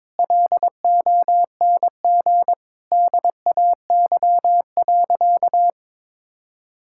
Morse code, heard as a keyed tone, about 22 wpm, 700 Hz. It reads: LONG DAY.
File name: practice / 22wpm / Sentences with 2-Word Phrases - 22wpm